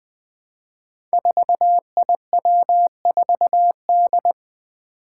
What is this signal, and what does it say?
Morse code, heard as a keyed tone, about 20 wpm, 700 Hz. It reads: 4IW4D
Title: Five Challenging Letter-Number Combinations 20wpm